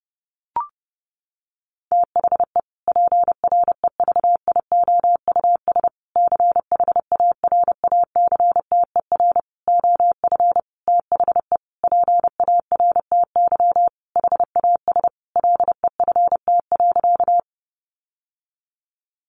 Morse code, heard as a keyed tone, about 30 wpm, 700 Hz. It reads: THE PREVIOUS CHARACTER OF THE PARTY HAS LEFT.
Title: Sentences from Top 1500 Words 30wpm